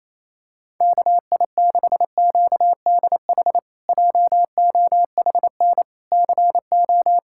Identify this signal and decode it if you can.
Morse code, heard as a keyed tone, about 27 wpm, 700 Hz. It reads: KI6QDH JOHN CO
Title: K1USN SST QSO Practice - 28wpm